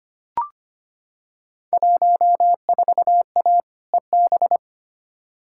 Morse code, heard as a keyed tone, about 25 wpm, 700 Hz. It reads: 14A EB
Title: Field Day Exchanges Encore - 25wpm